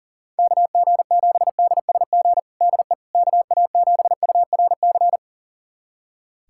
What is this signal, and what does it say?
Morse code, heard as a keyed tone, about 37 wpm, 690 Hz. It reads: KG7DSG DE KA7URC
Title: USA Callsign DE Callsign - 6 Characters - 40wpm